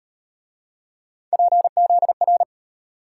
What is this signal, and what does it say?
Morse code, heard as a keyed tone, about 38 wpm, 700 Hz. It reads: PZR